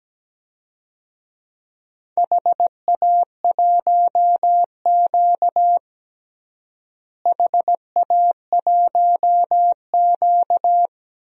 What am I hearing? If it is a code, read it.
Morse code, heard as a keyed tone, about 17 wpm, 700 Hz. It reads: HA1Q HA1Q
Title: DXCC-100 Call Signs - 4 Characters - Repeated 3x - 17wpm